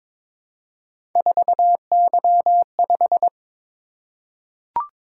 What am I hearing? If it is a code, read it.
Morse code, heard as a keyed tone, about 22 wpm, 700 Hz. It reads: 4Y5
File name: practice / 22wpm / Three Letter-Number Combinations - 22wpm